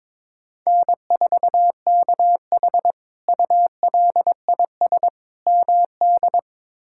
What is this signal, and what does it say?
Morse code, heard as a keyed tone, about 22 wpm, 700 Hz. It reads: N4KH ULIS MD
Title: K1USN SST QSO Practice - 22wpm